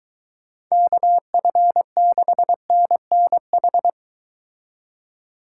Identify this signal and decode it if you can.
Morse code, heard as a keyed tone, about 23 wpm, 700 Hz. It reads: KF6NNH